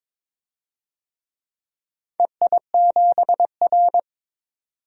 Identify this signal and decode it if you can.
Morse code, heard as a keyed tone, about 22 wpm, 700 Hz. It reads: EI7R
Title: DXCC-100 Call Signs - 4 Characters - 22wpm